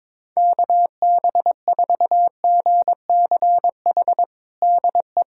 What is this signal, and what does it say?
Morse code, heard as a keyed tone, about 21 wpm, 700 Hz. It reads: KB4GCH DE
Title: USA Callsign DE Callsign 22wpm